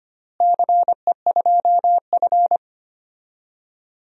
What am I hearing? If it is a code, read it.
Morse code, heard as a keyed tone, about 25 wpm, 700 Hz. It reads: CE2F
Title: DXCC-100 Call Signs - 4 Characters - Repeated 3x - 25wpm